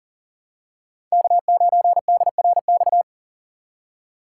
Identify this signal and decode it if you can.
Morse code, heard as a keyed tone, about 40 wpm, 700 Hz. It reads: K9DRX
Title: USA Call Signs - 5 Characters - 3x Character Spacing - 40wpm